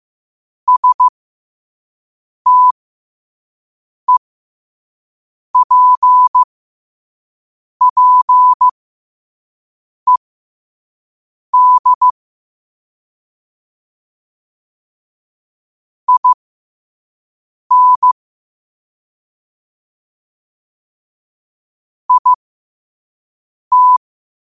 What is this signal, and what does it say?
Morse code, heard as a keyed tone, about 15 wpm, spaced as for 5 wpm, 1000 Hz. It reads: STEPPED IN IT
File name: qotdfiles / 5wpm